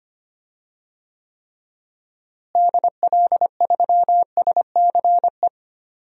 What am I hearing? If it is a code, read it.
Morse code, heard as a keyed tone, about 25 wpm, 700 Hz. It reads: DL3SCE
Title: DXCC-100 Call Signs - 6 Characters - Repeated 3x - 25wpm